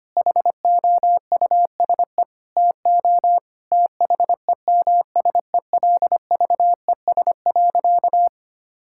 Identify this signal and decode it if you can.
Morse code, heard as a keyed tone, about 25 wpm, 700 Hz. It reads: HOUSE TO THEMSELVES.
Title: Sentences from Top 1000 Words - Review Part 5 - 25wpm